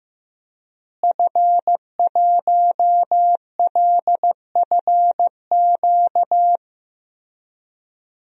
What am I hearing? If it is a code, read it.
Morse code, heard as a keyed tone, about 15 wpm, 700 Hz. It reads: F1LFQ
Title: DXCC-100 Call Signs - 5 Characters - Repeated 3x - 15wpm